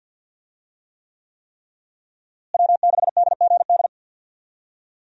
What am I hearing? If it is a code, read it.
Morse code, heard as a keyed tone, about 52 wpm, 700 Hz. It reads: W6DGD
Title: USA Call Signs - 5 Characters - Repeated 3x - 50wpm